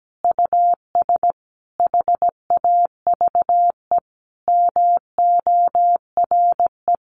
Morse code, heard as a keyed tone, about 17 wpm, 700 Hz. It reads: US HAVE MORE
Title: Sentences from Top 1300 Words 17wpm